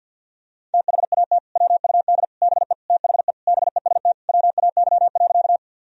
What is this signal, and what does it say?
Morse code, heard as a keyed tone, about 51 wpm, 700 Hz. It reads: THAT WUD BE THE BEST WAY.
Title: Sentences from Top 400 Words and Abbreviations 50wpm